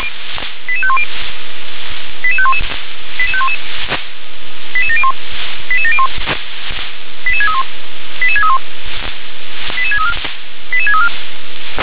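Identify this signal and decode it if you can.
Five-tone selective call (ZVEI), heard as quick runs of five tones, tones of about 70 ms. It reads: 88510 88511 88511 88701 88701 88631 88631 88634 88634
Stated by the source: Five-tone ZVEI selective call 88743 received on SDR.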